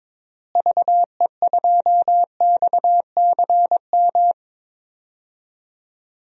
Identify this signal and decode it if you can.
Morse code, heard as a keyed tone, about 22 wpm, 700 Hz. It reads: VE2XCM
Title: DXCC-100 Call Signs - 6 Characters - Repeated 2x - 22wpm